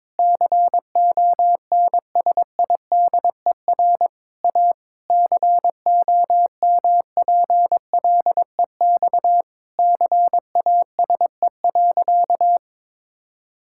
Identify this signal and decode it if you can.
Morse code, heard as a keyed tone, about 22 wpm, 700 Hz. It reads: CONSIDER A COMPLEX CASE.
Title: Sentences from Top 1000 Words - Review Part 6 - 22wpm